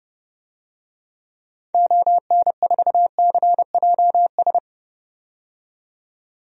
Morse code, heard as a keyed tone, about 30 wpm, 700 Hz. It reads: ON4CJS
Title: DXCC-100 Call Signs - 6 Characters - 30wpm